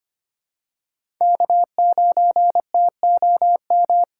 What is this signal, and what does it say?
Morse code, heard as a keyed tone, about 25 wpm, 700 Hz. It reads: K9TOM
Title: USA Callsign DE Callsign 25wpm